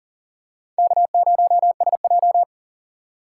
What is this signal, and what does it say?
Morse code, heard as a keyed tone, about 40 wpm, 700 Hz. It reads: K0SJ
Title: USA Call Signs - 4 Characters - 4x Character Spacing - 40wpm